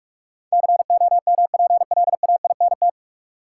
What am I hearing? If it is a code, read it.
Morse code, heard as a keyed tone, about 46 wpm, 700 Hz. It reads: COMPLAINT
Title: Sentences from Top 1400 Words - New Word Drill - 45wpm